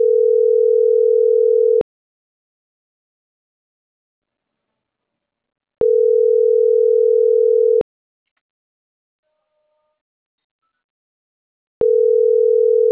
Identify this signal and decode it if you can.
Ringback tone (North America), 440 + 480 Hz.